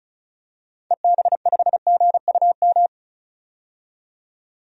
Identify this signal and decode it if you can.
Morse code, heard as a keyed tone, about 34 wpm, 700 Hz. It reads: EB5GUM
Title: DXCC-100 Call Signs - 6 Characters - 35wpm